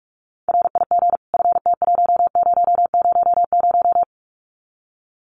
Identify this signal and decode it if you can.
Morse code, heard as a keyed tone, about 44 wpm, 700 Hz. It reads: RIG FT1000
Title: Common QSO Phrases 45wpm